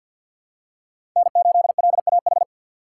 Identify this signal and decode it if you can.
Morse code, heard as a keyed tone, about 52 wpm, 700 Hz. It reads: N8LAH
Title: USA Call Signs - 5 Characters - 2x Character Spacing - 50wpm